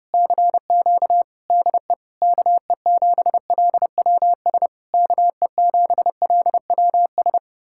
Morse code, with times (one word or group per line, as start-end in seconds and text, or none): 0.1-1.2 CQ
1.5-1.9 DE
2.2-4.7 KE7LWS
4.9-7.4 KE7LWS